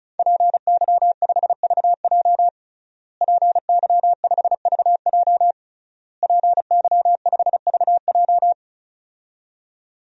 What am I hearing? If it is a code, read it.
Morse code, heard as a keyed tone, about 34 wpm, 700 Hz. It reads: PY5VJ PY5VJ PY5VJ
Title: DXCC-100 Call Signs - 5 Characters - Repeated 3x - 35wpm